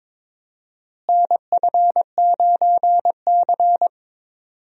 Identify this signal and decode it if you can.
Morse code, heard as a keyed tone, about 22 wpm, 700 Hz. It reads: NF9C
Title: USA Call Signs - 4 Characters - Repeated 3x - 22wpm